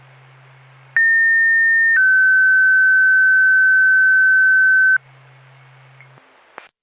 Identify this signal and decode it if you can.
Two-tone page, heard as a one-second tone, then a three-second tone, about 1800 then 1550 Hz. It is heard from 1.0 to 5.0 s.